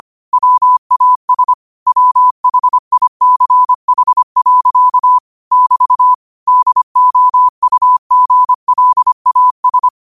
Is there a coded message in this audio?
Yes: Morse code from 0.3 to 9.9 s, heard as a keyed tone, about 25 wpm, 1000 Hz. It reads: WAS WHICH. <BT> DOUGLAS